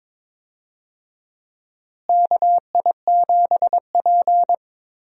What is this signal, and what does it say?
Morse code, heard as a keyed tone, about 22 wpm, 700 Hz. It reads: KI7P
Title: USA Call Signs - 4 Characters - Repeated 2x - 22wpm